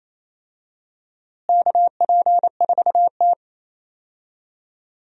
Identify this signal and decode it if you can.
Morse code, heard as a keyed tone, about 28 wpm, 700 Hz. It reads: KP4T
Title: DXCC-100 Call Signs - 4 Characters - Repeated 3x - 28wpm